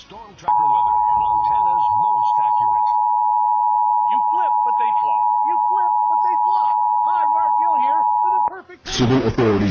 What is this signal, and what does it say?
Emergency Alert System (SAME) tones (the 853 and 960 Hz attention tone), with speech. It is heard from 0.4 to 8.5 s.
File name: hacked-eas-alert